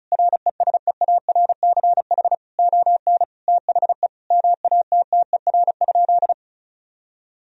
Morse code, heard as a keyed tone, about 36 wpm, 700 Hz. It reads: RESEARCH ON THE MATTER?
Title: Sentences from Top 1000 Words - Review Part 4 - 35wpm